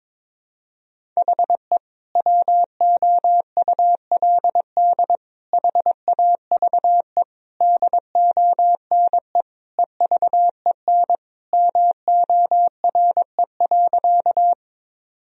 Morse code, heard as a keyed tone, about 22 wpm, 700 Hz. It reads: HE WOULD HAVE DONE EVEN MORE.
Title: Sentences from Top 1000 Words - Review Part 1 - 22wpm